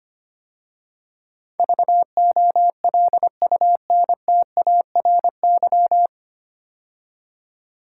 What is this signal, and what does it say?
Morse code, heard as a keyed tone, about 25 wpm, 700 Hz. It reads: VOLUNTARY
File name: practice / 25wpm / 9 Letter Words - First 3 Letters Given - 25wpm